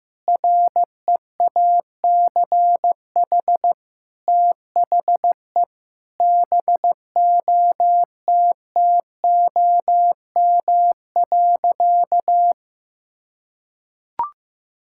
Morse code, heard as a keyed tone, about 16 wpm, 700 Hz. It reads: REACH THE BOTTOM.
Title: Sentences from Top 1000 Words - Review Part 6 - 15wpm